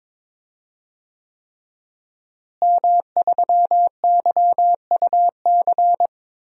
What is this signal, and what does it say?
Morse code, heard as a keyed tone, about 22 wpm, 700 Hz. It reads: M3YUC
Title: DXCC-100 Call Signs - 5 Characters - Repeated 3x - 22wpm